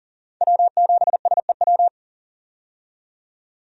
Morse code, heard as a keyed tone, about 40 wpm, 700 Hz. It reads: W7SEW
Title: USA Call Signs - 5 Characters - 40wpm